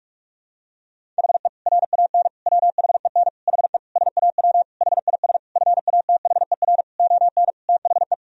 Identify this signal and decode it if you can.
Morse code, heard as a keyed tone, about 46 wpm, 700 Hz. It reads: HE RAN WHEN HE SAW HIS FATHER ON THE
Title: Sentences from Top 500 Words and Abbreviations 45wpm